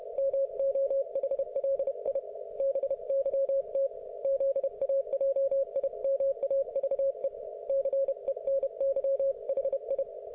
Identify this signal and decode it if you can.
Morse code, heard as a keyed tone, about 31 wpm, 530 Hz. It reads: MOHLI BYT ZAJIMAVE CENY HI